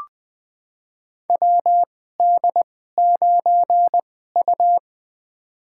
Morse code, heard as a keyed tone, about 21 wpm, spaced as for 15 wpm, 700 Hz. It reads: WD9U
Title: USA Call Signs - 2x Character Spacing - 20wpm